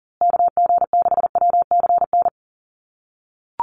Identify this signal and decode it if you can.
Morse code, heard as a keyed tone, about 40 wpm, 700 Hz. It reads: KG6WCN
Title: USA Call Signs Repeated 3x 40wpm